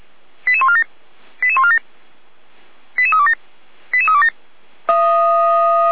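Five-tone selective call (ZVEI), heard as quick runs of five tones, tones of about 70 ms. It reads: 88147 88147 88327 88327
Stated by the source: ZVEI five-tone selective call 88397 received on SDR.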